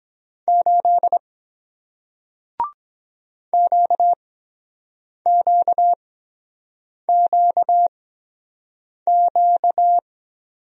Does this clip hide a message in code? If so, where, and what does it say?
Yes: Morse code from 0.5 to 10.0 s, heard as a keyed tone, about 23 wpm, spaced as for 7 wpm, 700 Hz. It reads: 8 QQQQ